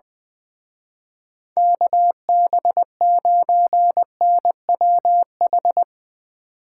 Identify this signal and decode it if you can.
Morse code, heard as a keyed tone, about 20 wpm, 700 Hz. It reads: KB9NWH
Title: Speed Racing - USA 2x3 Call Signs 20wpm